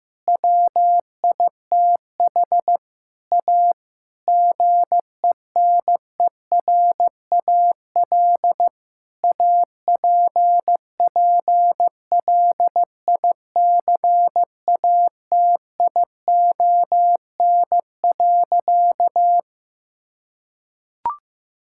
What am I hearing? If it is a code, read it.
Morse code, heard as a keyed tone, about 15 wpm, 700 Hz. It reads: WITH A GENERAL APPLICATION.
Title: Sentences from Top 1600 Words 15wpm